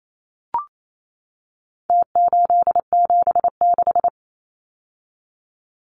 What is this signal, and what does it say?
Morse code, heard as a keyed tone, about 28 wpm, 700 Hz. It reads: T876
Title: Numbers - Contest Style - 4-Digit 28wpm